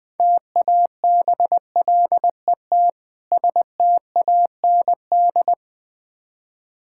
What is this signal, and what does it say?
Morse code, heard as a keyed tone, about 20 wpm, 690 Hz. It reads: TABLET STAND
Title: Difficult Words with Hints - 20wpm